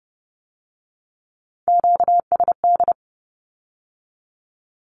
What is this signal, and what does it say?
Morse code, heard as a keyed tone, about 30 wpm, 700 Hz. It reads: QSD
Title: Q-Codes with Spoken Meaning - 30wpm